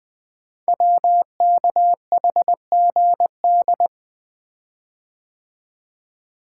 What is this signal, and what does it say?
Morse code, heard as a keyed tone, about 20 wpm, 700 Hz. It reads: WKHGD
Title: Code Groups - 5 Letters - 20wpm